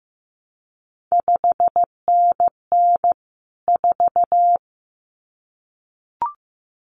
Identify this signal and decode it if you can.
Morse code, heard as a keyed tone, about 15 wpm, 700 Hz. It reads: 5NN 4
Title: CQ WW DX Contest Exchange - 15wpm